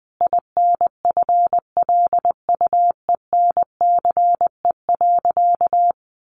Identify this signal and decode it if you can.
Morse code, heard as a keyed tone, about 19 wpm, 700 Hz. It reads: INFLUENCE.
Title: Sentences from Top 1100 Words 20wpm